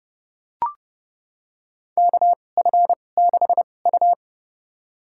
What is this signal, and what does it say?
Morse code, heard as a keyed tone, about 30 wpm, spaced as for 22 wpm, 700 Hz. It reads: KF6U
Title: USA Call Signs - 4 Characters - 2x Character Spacing - 30wpm